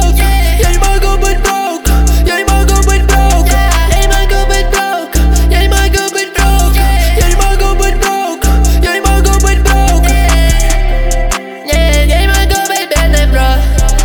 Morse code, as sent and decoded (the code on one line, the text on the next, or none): none